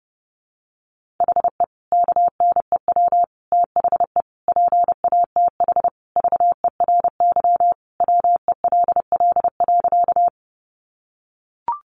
Morse code, heard as a keyed tone, about 30 wpm, 700 Hz. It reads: HE KNEW THE PATH VERY WELL.
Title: Sentences from Top 1400 Words 30wpm